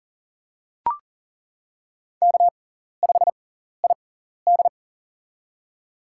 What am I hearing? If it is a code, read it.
Morse code, heard as a keyed tone, about 40 wpm, spaced as for 14 wpm, 700 Hz. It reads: K5ID
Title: USA Call Signs - 4 Characters - 6x Character Spacing - 40wpm